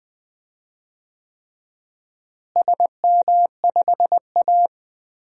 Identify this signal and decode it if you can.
Morse code, heard as a keyed tone, about 20 wpm, 700 Hz. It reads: SM5A
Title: DXCC-100 Call Signs - 4 Characters - Repeated 3x - 20wpm